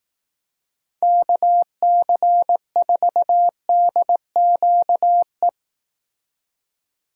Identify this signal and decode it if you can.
Morse code, heard as a keyed tone, about 18 wpm, 700 Hz. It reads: KC4DQE
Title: Speed Racing - USA 2x3 Call Signs 15wpm